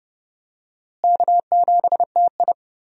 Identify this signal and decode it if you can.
Morse code, heard as a keyed tone, about 30 wpm, 700 Hz. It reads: K7TI
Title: USA Call Signs - 4 Characters - Repeated 3x - 30wpm